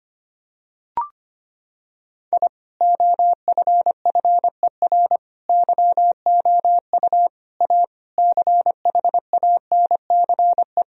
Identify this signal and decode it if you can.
Morse code, heard as a keyed tone, about 25 wpm, 700 Hz. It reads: I OFFER YOU A CHANCE
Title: Sentences from Top 1000 Words - Review Part 4 - 25wpm